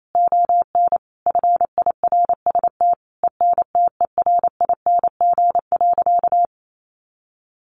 Morse code, heard as a keyed tone, about 28 wpm, 700 Hz. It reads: ON FIRST ENTERING.
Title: A Sound Of Thunder 28wpm